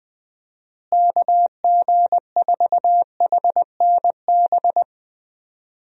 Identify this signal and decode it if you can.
Morse code, heard as a keyed tone, about 19 wpm, 700 Hz. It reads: KG4HNB
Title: USA Call Signs - 6 Characters - Repeated 3x - 20wpm